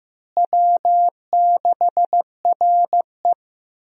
Morse code, heard as a keyed tone, about 15 wpm, 700 Hz. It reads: W6RE
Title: USA Call Signs - 4 Characters - Repeated 3x - 15wpm